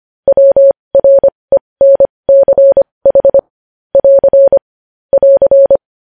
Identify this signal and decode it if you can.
Morse code, heard as a keyed tone, about 25 wpm, spaced as for 20 wpm, 550 Hz. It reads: WRENCH <AR> <AR>
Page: news headlines in morse code at 20 wpm